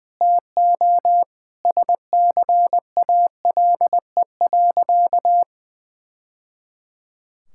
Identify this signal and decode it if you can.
Morse code, heard as a keyed tone, about 20 wpm, 700 Hz. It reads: TO SCALE.
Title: Sentences from Top 1000 Words - Review Part 5 - 20wpm